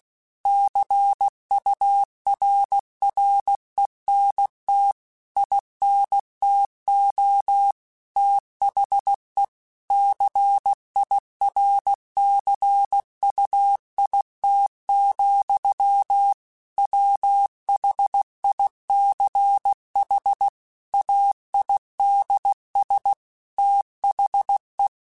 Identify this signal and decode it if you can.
Morse code, heard as a keyed tone, about 16 wpm, 800 Hz. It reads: CURRENT INTO THE CIRCUIT, WHICH AIDS THE